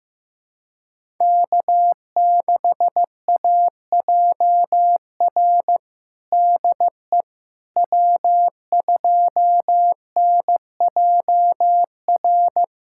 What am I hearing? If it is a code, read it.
Morse code, heard as a keyed tone, about 15 wpm, 700 Hz. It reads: K6AJR DE W2NJR
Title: USA Callsign DE Callsign - 5 Characters - 15wpm